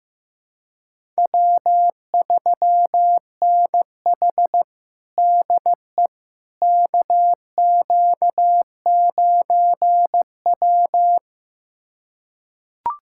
Morse code, heard as a keyed tone, about 15 wpm, 700 Hz. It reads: W3NH DE KQ9W
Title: USA Callsign DE Callsign - 4 Characters - 15wpm